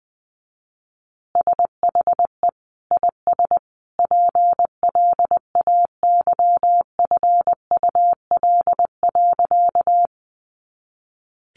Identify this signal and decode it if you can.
Morse code, heard as a keyed tone, about 20 wpm, 700 Hz. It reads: SHE IS PLAYFUL.
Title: Easy Phrases 20wpm